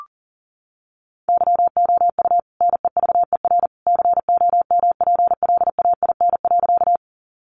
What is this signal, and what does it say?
Morse code, heard as a keyed tone, about 40 wpm, 700 Hz. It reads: YOU NEVER COMPLAIN.
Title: Sentences from Top 1300 Words 40wpm